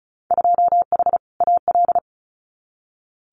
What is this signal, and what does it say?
Morse code, heard as a keyed tone, about 36 wpm, 690 Hz. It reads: 2H AL
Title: Winter Field Day Exchanges 35wpm